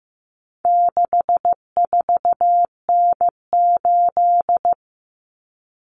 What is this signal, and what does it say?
Morse code, heard as a keyed tone, about 15 wpm, 700 Hz. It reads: 64N8
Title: Common Cut Numbers - 4-Digit 15wpm